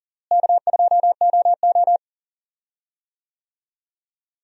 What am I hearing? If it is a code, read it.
Morse code, heard as a keyed tone, about 40 wpm, 700 Hz. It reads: K2OO
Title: USA Call Signs - 4 Characters - Repeated 2x - 40wpm